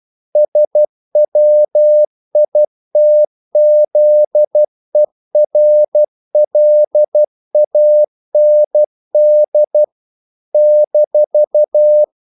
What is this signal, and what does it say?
Morse code, heard as a keyed tone, about 12 wpm, 600 Hz. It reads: SWITZERLAND -
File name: CW / Practice / NationalCapitals12